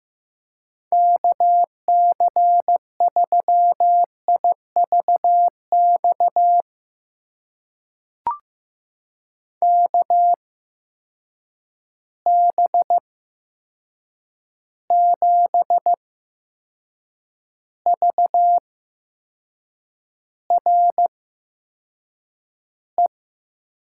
Morse code, heard as a keyed tone, about 15 wpm, spaced as for 6 wpm, 700 Hz. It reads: KC3IVX KB7VRE
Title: USA Call Signs - 6 Characters - 8x Character Spacing - 15wpm